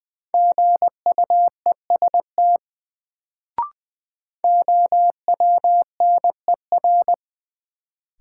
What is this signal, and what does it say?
Morse code, heard as a keyed tone, about 20 wpm, 700 Hz. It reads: GUEST OWNER
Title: Sentences from Top 1600 Words - New Word Drill Alt - 20wpm